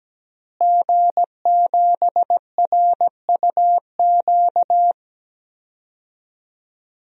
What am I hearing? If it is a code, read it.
Morse code, heard as a keyed tone, about 17 wpm, 700 Hz. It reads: G7RUQ
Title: DXCC-100 Call Signs - 5 Characters - Repeated 2x - 17wpm